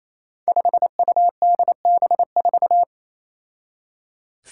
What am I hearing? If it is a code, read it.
Morse code, heard as a keyed tone, about 27 wpm, 700 Hz. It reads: 5UDB4
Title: Five Challenging Letter-Number Combinations 28wpm